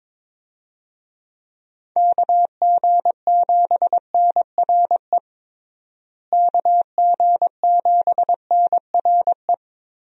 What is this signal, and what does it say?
Morse code, heard as a keyed tone, about 22 wpm, 700 Hz. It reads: KG7NRE KG7NRE
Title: USA Call Signs - 6 Characters - Repeated 3x - 22wpm